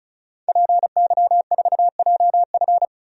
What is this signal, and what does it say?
Morse code, heard as a keyed tone, about 36 wpm, 700 Hz. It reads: PY4JF
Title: DXCC-100 Call Signs - 5 Characters - 35wpm